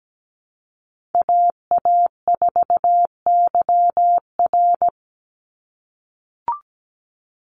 Morse code, heard as a keyed tone, about 17 wpm, 700 Hz. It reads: AA4YR
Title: USA Call Signs 17wpm